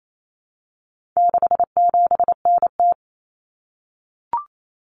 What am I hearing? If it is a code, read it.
Morse code, heard as a keyed tone, about 28 wpm, 700 Hz. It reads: 67NT
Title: Common Cut Numbers - 4-Digit 28wpm